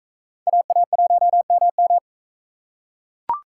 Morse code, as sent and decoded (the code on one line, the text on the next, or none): .- .- .---- -- --
AA1MM